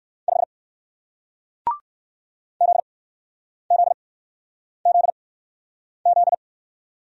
Morse code, heard as a keyed tone, about 57 wpm, spaced as for 9 wpm, 700 Hz. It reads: 5 ZZZZ